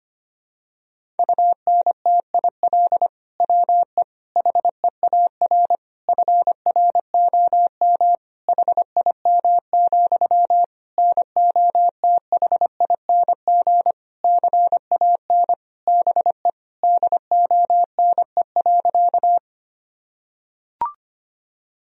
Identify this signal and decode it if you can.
Morse code, heard as a keyed tone, about 25 wpm, 700 Hz. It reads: UNTIL WE HEAR FROM HIM, NOTHING CAN BE DONE.
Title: Sentences from Top 1000 Words - Review Part 1 - 25wpm